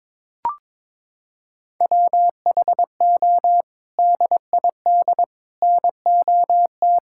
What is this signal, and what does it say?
Morse code, heard as a keyed tone, about 21 wpm, 700 Hz. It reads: WHO DID NOT
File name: practice / 22wpm / Sets of 7 Words - 22wpm